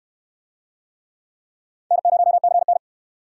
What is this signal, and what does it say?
Morse code, heard as a keyed tone, about 62 wpm, 700 Hz. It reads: N0CN